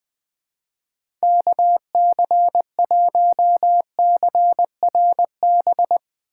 Morse code, heard as a keyed tone, about 21 wpm, 700 Hz. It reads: KC1CRB
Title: USA Call Signs - 6 Characters - 2x Character Spacing - 20wpm